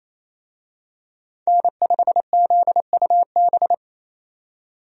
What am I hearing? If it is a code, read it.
Morse code, heard as a keyed tone, about 28 wpm, 700 Hz. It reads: N5ZUB
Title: USA Call Signs - 5 Characters - 28wpm